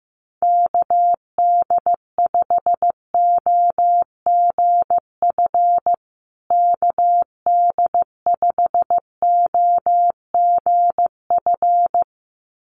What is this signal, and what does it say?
Morse code, heard as a keyed tone, about 15 wpm, 700 Hz. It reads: KD5OGF KD5OGF
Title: USA Call Signs Repeated 2x 15wpm